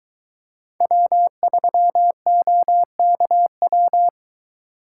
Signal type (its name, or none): Morse code